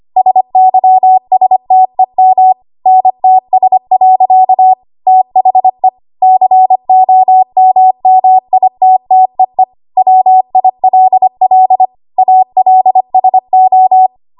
Morse code, heard as a keyed tone, about 25 wpm, 750 Hz. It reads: SYSTEM NTS. THE COMMITTEE WILL ALSO
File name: Archive / 25 WPM / 25WPM